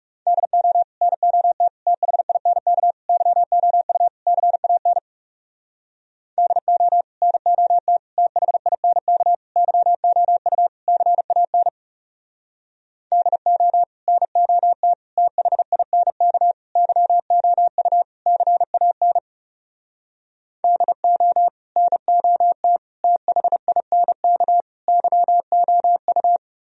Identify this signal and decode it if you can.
Morse code, heard as a keyed tone, about 37 wpm, 700 Hz. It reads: DO NOT THINK YOU CAN DO NOT THINK YOU CAN DO NOT THINK YOU CAN DO NOT THINK YOU
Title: Speed-Racing - Sets of 5 Words 30wpm